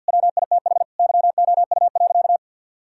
Morse code, heard as a keyed tone, about 49 wpm, 700 Hz. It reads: WITH YOU.
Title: Sentences from Top 1000 Words - Review Part 5 - 50wpm